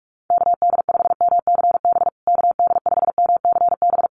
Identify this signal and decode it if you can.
Morse code, heard as a keyed tone, about 44 wpm, 700 Hz. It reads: KD5MCB KD5MCB
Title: USA Call Signs Repeated 3x 45wpm